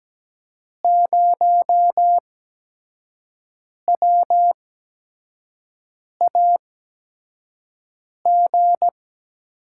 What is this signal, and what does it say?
Morse code, heard as a keyed tone, about 17 wpm, spaced as for 5 wpm, 700 Hz. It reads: 0WAG
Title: USA Call Signs - 6 Characters - 8x Character Spacing - 17wpm